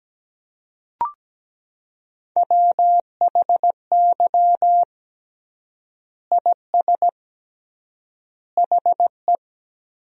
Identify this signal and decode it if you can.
Morse code, heard as a keyed tone, about 17 wpm, 700 Hz. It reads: WHY IS HE
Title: 3x Word-Spacing - Sentences from Top 200 Words - 17wpm